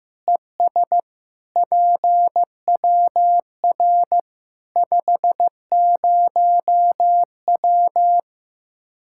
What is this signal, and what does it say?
Morse code, heard as a keyed tone, about 15 wpm, 700 Hz. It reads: ES PWR 50W
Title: Practice QSO - Exchange 2 - Info - 15wpm